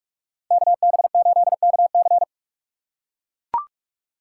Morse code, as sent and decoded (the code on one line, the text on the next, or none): -.- -... ---.. -.- -.-.
KB8KC